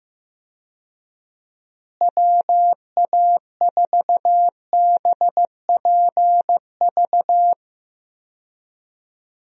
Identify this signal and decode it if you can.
Morse code, heard as a keyed tone, about 15 wpm, 700 Hz. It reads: WA4BPV
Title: USA Call Signs - 6 Characters - Repeated 3x - 15wpm